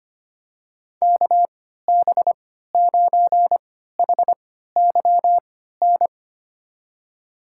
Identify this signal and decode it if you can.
Morse code, heard as a keyed tone, about 25 wpm, spaced as for 14 wpm, 700 Hz. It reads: KB9HYN
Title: USA Call Signs - 6 Characters - 3x Character Spacing - 25wpm